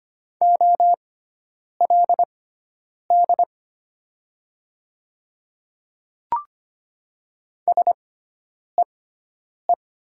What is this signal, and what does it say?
Morse code, heard as a keyed tone, about 25 wpm, spaced as for 9 wpm, 700 Hz. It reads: OLD SEE
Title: 6x Character Spacing - Top 200 Words - 25wpm